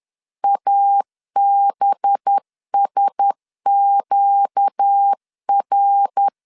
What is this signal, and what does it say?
Morse code, heard as a keyed tone, about 10 wpm, 800 Hz. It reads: ABSQR
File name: LEZIONI / 7 GF9Z / 7 GF9 10wpm 2747